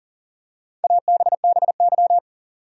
Morse code, heard as a keyed tone, about 40 wpm, 700 Hz. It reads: ABBY
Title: USA Common Names - Max 5 Letters - 40wpm